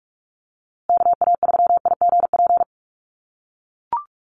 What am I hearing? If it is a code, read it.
Morse code, heard as a keyed tone, about 44 wpm, 700 Hz. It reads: KA3IGP
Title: USA Call Signs Repeated 2x 45wpm